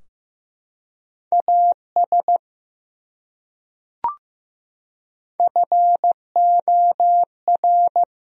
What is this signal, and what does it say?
Morse code, heard as a keyed tone, about 16 wpm, 700 Hz. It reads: AS FOR
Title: Top 100 Words Repeated 15wpm